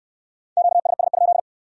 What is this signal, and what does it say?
Morse code, heard as a keyed tone, about 62 wpm, 700 Hz. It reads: QSS?